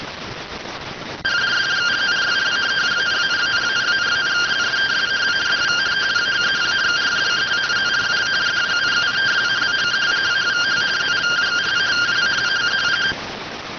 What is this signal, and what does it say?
Two-tone teleprinter signal, about 170 Hz shift.